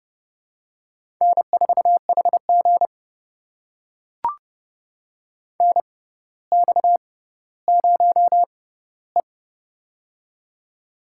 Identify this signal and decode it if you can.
Morse code, heard as a keyed tone, about 31 wpm, spaced as for 15 wpm, 700 Hz. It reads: N4HG NX0E
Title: USA Call Signs - 4 Characters - 6x Character Spacing - 30wpm